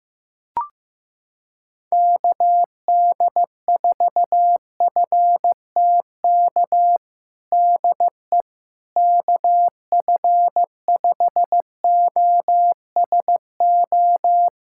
Morse code, heard as a keyed tone, about 15 wpm, 700 Hz. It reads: KD4FTK DE KF5OSO
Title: USA Callsign DE Callsign 15wpm